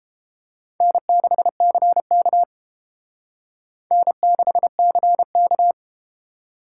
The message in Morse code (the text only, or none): N6CK N6CK